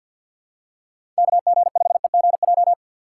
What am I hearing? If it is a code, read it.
Morse code, heard as a keyed tone, about 52 wpm, 700 Hz. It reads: KG5EGJ
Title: USA Call Signs - 6 Characters - 3x Character Spacing - 50wpm